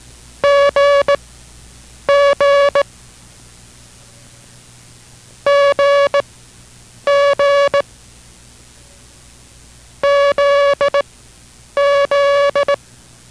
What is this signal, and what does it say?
Morse code, heard as a keyed tone, about 18 wpm, spaced as for 8 wpm, 570 Hz. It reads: GG GG ZZ